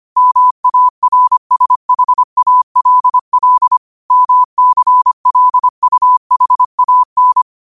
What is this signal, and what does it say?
Morse code, heard as a keyed tone, about 25 wpm, 1000 Hz. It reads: MARSHALL MCLUHAN
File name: qotdfiles / 25wpm